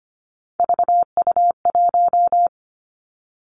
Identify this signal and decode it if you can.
Morse code, heard as a keyed tone, about 25 wpm, 700 Hz. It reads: VU1
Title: Three Challenging Letter-Number Combinations 25wpm